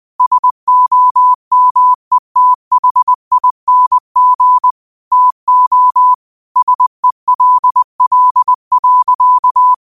Morse code, heard as a keyed tone, about 20 wpm, 1000 Hz. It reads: SOMETHING TO SELL.
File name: qotdfiles / 20wpm